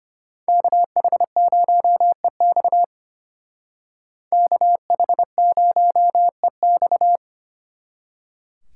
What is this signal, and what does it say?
Morse code, heard as a keyed tone, about 32 wpm, 700 Hz. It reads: KH0EX KH0EX